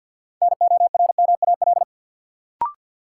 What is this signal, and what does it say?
Morse code, heard as a keyed tone, about 51 wpm, 700 Hz. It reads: NORMAL
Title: Sentences from Top 1200 Words - New Word Drill Alt - 50wpm